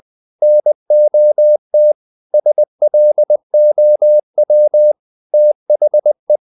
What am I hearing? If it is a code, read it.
Morse code, heard as a keyed tone, about 20 wpm, 600 Hz. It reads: NOT SLOW THE
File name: the old man and the sea / the old man and the sea part 1 20wpm